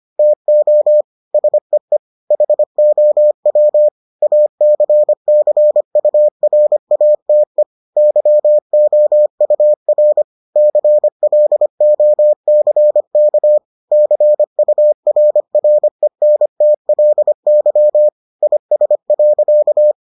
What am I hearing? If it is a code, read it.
Morse code, heard as a keyed tone, about 25 wpm, 600 Hz. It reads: TO SEE HOW ACCURATE YOUR CLOCK CURRENTLY IS.